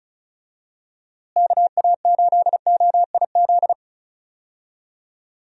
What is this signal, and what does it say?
Morse code, heard as a keyed tone, about 36 wpm, 700 Hz. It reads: KA8OIZ
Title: USA Call Signs - 6 Characters - Repeated 3x - 35wpm